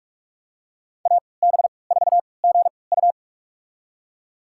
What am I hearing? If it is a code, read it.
Morse code, heard as a keyed tone, about 46 wpm, spaced as for 25 wpm, 700 Hz. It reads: AB4GU
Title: USA Call Signs - 5 Characters - 3x Character Spacing - 45wpm